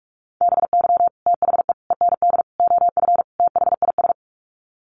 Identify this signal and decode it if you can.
Morse code, heard as a keyed tone, about 44 wpm, 700 Hz. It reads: BY THE END OF THIS
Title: Sets of 5 Words Encore 45wpm